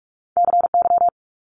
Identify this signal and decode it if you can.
Morse code, heard as a keyed tone, about 46 wpm, 700 Hz. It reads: CY
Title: Multi-letter Phonemes 45wpm